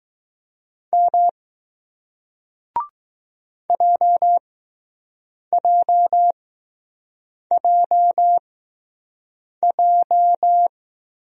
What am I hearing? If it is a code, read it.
Morse code, heard as a keyed tone, about 20 wpm, spaced as for 6 wpm, 700 Hz. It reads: M JJJJ